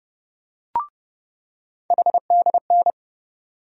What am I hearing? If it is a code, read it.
Morse code, heard as a keyed tone, about 32 wpm, 700 Hz. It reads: HDN
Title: USA Call Sign Postfixes - 3 Letters 30wpm